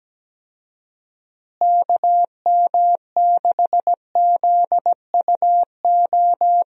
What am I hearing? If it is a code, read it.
Morse code, heard as a keyed tone, about 17 wpm, 700 Hz. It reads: KM6ZUO
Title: USA Call Signs - 6 Characters - Repeated 2x - 17wpm